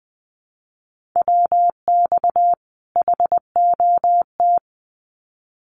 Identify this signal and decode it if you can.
Morse code, heard as a keyed tone, about 20 wpm, 700 Hz. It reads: WX HOT
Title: Common QSO Phrases 20wpm